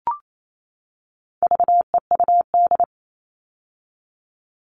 Morse code, heard as a keyed tone, about 28 wpm, 700 Hz. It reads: VEUD
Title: All Cut Numbers - 4-Digit 28wpm